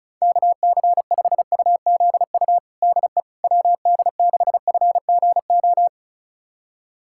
Morse code, heard as a keyed tone, about 36 wpm, 690 Hz. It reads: KC5UZU DE WD6FGO
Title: USA Callsign DE Callsign 35wpm